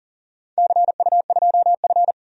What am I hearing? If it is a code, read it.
Morse code, heard as a keyed tone, about 39 wpm, 700 Hz. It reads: CU2F
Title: DXCC-100 Call Signs - 4 Characters - 40wpm